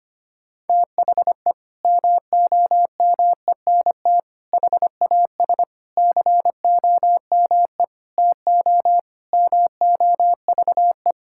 Morse code, heard as a keyed tone, about 25 wpm, 700 Hz. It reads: THE MOMENT HAS COME TO MOVE
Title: Sentences from Top 1000 Words - Review Part 5 - 25wpm